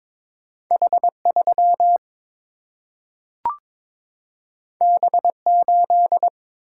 Morse code, heard as a keyed tone, about 22 wpm, 700 Hz. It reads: H3 B8
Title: Single-Letter Single-Number 22wpm